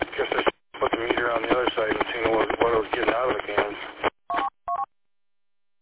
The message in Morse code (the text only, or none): none